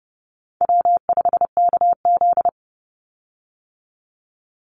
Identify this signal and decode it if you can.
Morse code, heard as a keyed tone, about 30 wpm, 700 Hz. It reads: W5KZ
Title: USA Call Signs 30wpm